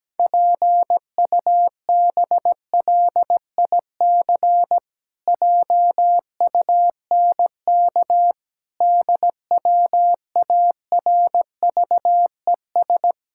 Morse code, heard as a keyed tone, about 17 wpm, 700 Hz. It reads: PUBLIC JUNK DWARVES
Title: Pangram Challenge v2 - 17wpm